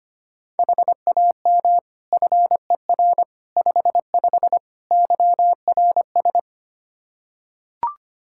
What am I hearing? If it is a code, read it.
Morse code, heard as a keyed tone, about 25 wpm, 700 Hz. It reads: HAM FER 55 YRS
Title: Practice QSO - Exchange 3 - More Info - 25wpm